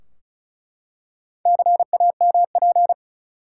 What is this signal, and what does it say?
Morse code, heard as a keyed tone, about 36 wpm, 700 Hz. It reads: CAMP
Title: Four Letter Words - 35wpm